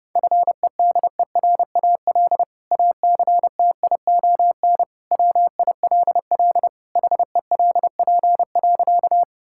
Morse code, heard as a keyed tone, about 30 wpm, 700 Hz. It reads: FEDERAL ACTION WILL HELP.
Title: Sentences from Top 1000 Words - Review Part 5 - 30wpm